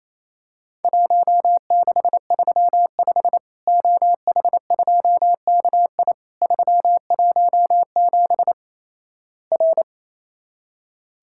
Morse code, heard as a keyed tone, about 28 wpm, 700 Hz. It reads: 1635 OH2KI 317 R
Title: Worked All Europe DX Contest - QTC Exchange - 28wpm